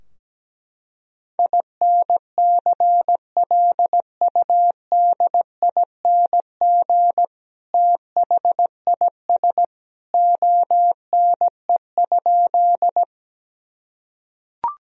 Morse code, heard as a keyed tone, about 17 wpm, 700 Hz. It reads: INCLUDING THIS ONE?
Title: Sentences from Top 1000 Words - Review Part 4 - 17wpm